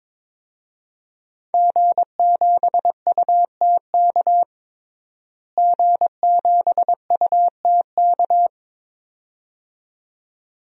Morse code, heard as a keyed tone, about 22 wpm, 700 Hz. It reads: G7UTK G7UTK
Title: DXCC-100 Call Signs - 5 Characters - Repeated 2x - 22wpm